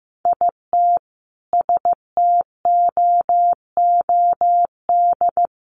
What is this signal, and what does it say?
Morse code, heard as a keyed tone, about 15 wpm, 700 Hz. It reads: IT STOOD
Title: Sentences from Top 500 Words 15wpm